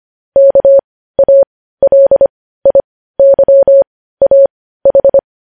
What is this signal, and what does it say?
Morse code, heard as a keyed tone, about 25 wpm, spaced as for 15 wpm, 550 Hz. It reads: KALIYAH
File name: podcast / News headlines-1x-25x15